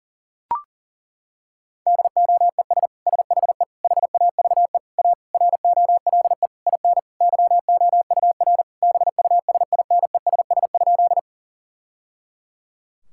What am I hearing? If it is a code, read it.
Morse code, heard as a keyed tone, about 40 wpm, 700 Hz. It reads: DOES SHE HAVE A ROLE IN YOUR BUSINESS?
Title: Sentences from Top 1000 Words - Review Part 6 - 40wpm